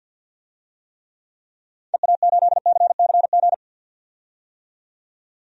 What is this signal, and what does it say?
Morse code, heard as a keyed tone, about 46 wpm, 700 Hz. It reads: EA8CCG
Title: DXCC-100 Call Signs - 6 Characters - 50wpm